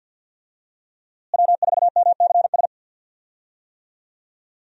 Morse code, heard as a keyed tone, about 52 wpm, 700 Hz. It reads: W4MCS
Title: USA Call Signs - 5 Characters - Repeated 3x - 50wpm